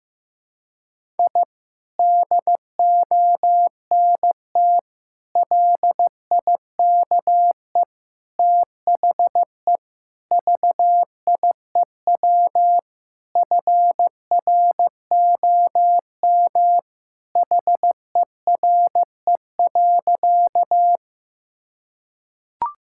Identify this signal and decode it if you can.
Morse code, heard as a keyed tone, about 15 wpm, 700 Hz. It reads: I DONT LIKE THE VIEW FROM HERE.
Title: Sentences from Top 1000 Words - Review Part 6 - 15wpm